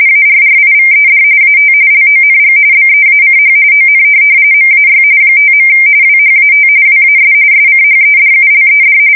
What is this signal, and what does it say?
RTTY, about 45.45 baud. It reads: ARRIVED AT LONG BEACH ON MAY 5. WHAT WAS SCHEDULED
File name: fsk170 / USS-New-Jersey